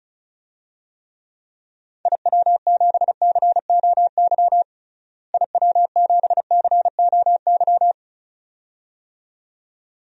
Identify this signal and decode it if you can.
Morse code, heard as a keyed tone, about 34 wpm, 700 Hz. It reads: IW7COY IW7COY
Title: DXCC-100 Call Signs - 6 Characters - Repeated 2x - 35wpm